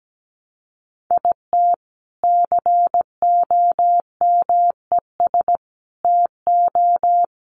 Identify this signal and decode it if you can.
Morse code, heard as a keyed tone, about 17 wpm, 700 Hz. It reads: IT COMES TO
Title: Sets of 3 Words 17wpm